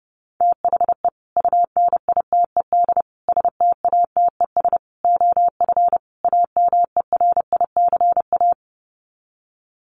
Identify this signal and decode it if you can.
Morse code, heard as a keyed tone, about 30 wpm, 700 Hz. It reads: THE UNITED STATES OF AMERICA
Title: Sets of 5 Words 30wpm